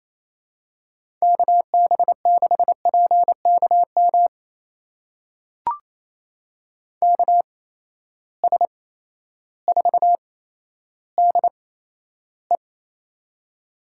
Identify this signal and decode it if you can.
Morse code, heard as a keyed tone, about 27 wpm, spaced as for 13 wpm, 700 Hz. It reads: KB6PKM KS4DE
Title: USA Call Signs - 8x Character Spacing - 28wpm